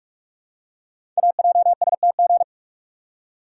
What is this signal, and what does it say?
Morse code, heard as a keyed tone, about 44 wpm, 700 Hz. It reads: AJSTG
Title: Code Groups - 5 Letters - 45wpm